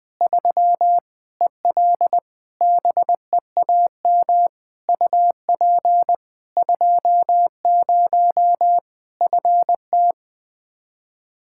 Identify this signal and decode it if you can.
Morse code, heard as a keyed tone, about 21 wpm, 700 Hz. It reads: 3 EL BEAM UP 20 FT
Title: Practice QSO - Exchange 2 - Info - 20wpm